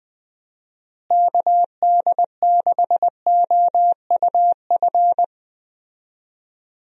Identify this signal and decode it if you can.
Morse code, heard as a keyed tone, about 21 wpm, 700 Hz. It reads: KD6OUF
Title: USA Call Signs - 6 Characters - Repeated 3x - 20wpm